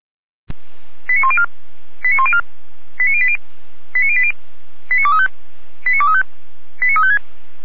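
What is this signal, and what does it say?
Five-tone selective call (ZVEI), heard as quick runs of five tones, tones of about 70 ms. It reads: 89104 89104 89080 89080 89235 89235 89356
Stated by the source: ZVEI five-tone selective call received on an SDR.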